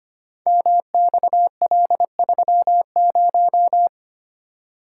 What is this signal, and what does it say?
Morse code, heard as a keyed tone, about 25 wpm, 700 Hz. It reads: MXL30
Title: Code Groups - 5 Letter:Numbers - 25wpm